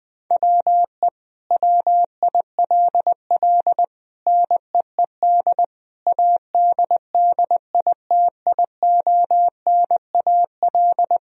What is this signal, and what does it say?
Morse code, heard as a keyed tone, about 20 wpm, 700 Hz. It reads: WE WILL NEED ADDITIONAL
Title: Sentences from Top 1000 Words - Review Part 4 - 20wpm